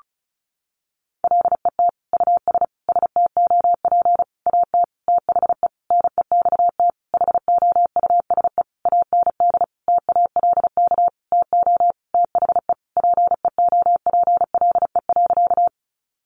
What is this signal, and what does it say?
Morse code, heard as a keyed tone, about 34 wpm, 700 Hz. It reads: LET US STOP AT THE NEXT HOUSE AND TALK TO THE PEOPLE.
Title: Sentences from Top 400 Words 35wpm